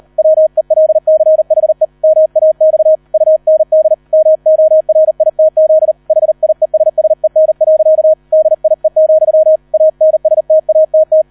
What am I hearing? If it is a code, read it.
Morse code, heard as a keyed tone, about 38 wpm, 630 Hz, with radio noise. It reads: WELCHE MAX UND MORITZ HIESSEN. DIE, ANSTATT